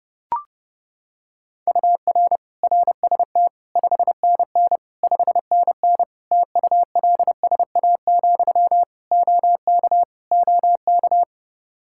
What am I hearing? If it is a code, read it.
Morse code, heard as a keyed tone, about 29 wpm, 700 Hz. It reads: UR RST 5NN 5NN TULSA, OK OK
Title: Practice QSO - Exchange 1 - Beginning - 30wpm